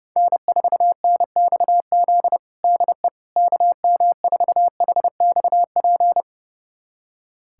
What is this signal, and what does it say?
Morse code, heard as a keyed tone, about 30 wpm, 700 Hz. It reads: N4NXZ DE KM4HXP
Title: USA Callsign DE Callsign 30wpm